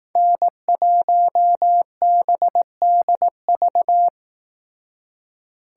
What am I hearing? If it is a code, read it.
Morse code, heard as a keyed tone, about 18 wpm, 700 Hz. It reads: N1BDV
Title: Speed Racing - USA 1x3 Call Signs 15wpm